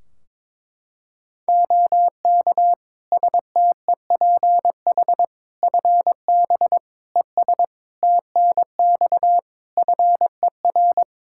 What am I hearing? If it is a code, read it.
Morse code, heard as a keyed tone, about 22 wpm, 700 Hz. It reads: OK STEPH FB ES TNX FER
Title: Practice QSO - Exchange 2 - Info - 22wpm